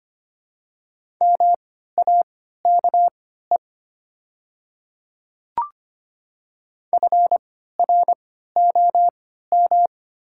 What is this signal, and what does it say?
Morse code, heard as a keyed tone, about 25 wpm, spaced as for 14 wpm, 700 Hz. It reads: MAKE FROM
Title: 3x Character Spacing - Top 100 Words - 25wpm